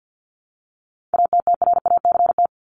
Morse code, heard as a keyed tone, about 50 wpm, 700 Hz. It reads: ATTRACT
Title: Top 1501-1750 Words 50wpm